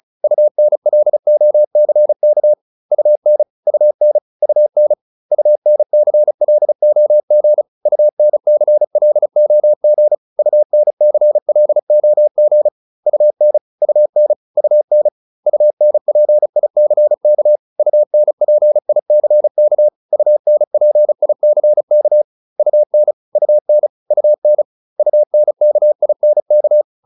Morse code, heard as a keyed tone, about 36 wpm, 600 Hz. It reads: UNLOCK UN UN UN UNCLOG UNCLOG UNCLOG UN UN UN UNPICK UNPICK UNPICK UN UN UN UNKINK